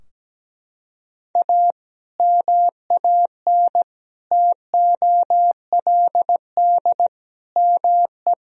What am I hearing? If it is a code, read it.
Morse code, heard as a keyed tone, about 17 wpm, 700 Hz. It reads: A MAN TOLD ME
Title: Sentences from Top 1000 Words - Review Part 3 - 17wpm